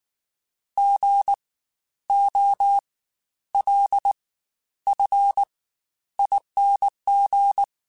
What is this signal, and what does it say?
Morse code, heard as a keyed tone, about 19 wpm, spaced as for 11 wpm, 800 Hz. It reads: GOLFING
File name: Spaced Alphas / Spaced Alphas mp3s / Gs sp @18